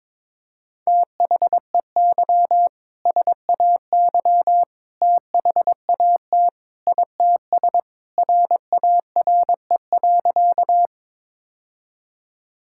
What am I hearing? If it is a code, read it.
Morse code, heard as a keyed tone, about 22 wpm, 700 Hz. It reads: THEY SAY THAT ITS RARE.
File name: practice / 22wpm / Sentences with 2-Word Phrases - 22wpm